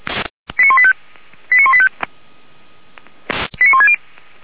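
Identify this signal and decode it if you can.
Five-tone selective call (ZVEI), heard as quick runs of five tones, tones of about 70 ms. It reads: 89185 89185 89160
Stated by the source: ZVEI five-tone selective call received on SDR, from fire brigade feuerwehr-orlamuende.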